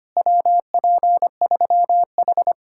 Morse code, heard as a keyed tone, about 29 wpm, 700 Hz. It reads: WP3H
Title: Speed-Racing Common QSO Phrases - 25wpm